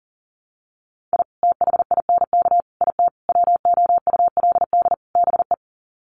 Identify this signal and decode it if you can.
Morse code, heard as a keyed tone, about 40 wpm, 700 Hz. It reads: I THINK IT WOULD BE
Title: Sets of 5 Words 40wpm